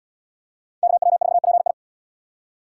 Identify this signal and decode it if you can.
Morse code, heard as a keyed tone, about 77 wpm, 700 Hz. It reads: KW4PI